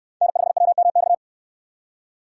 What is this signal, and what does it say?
Morse code, heard as a keyed tone, about 65 wpm, 700 Hz. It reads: N5WMX